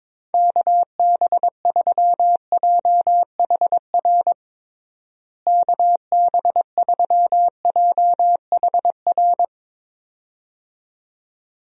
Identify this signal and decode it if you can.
Morse code, heard as a keyed tone, about 22 wpm, 700 Hz. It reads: KB3JHR KB3JHR
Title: USA Call Signs - 6 Characters - Repeated 2x - 22wpm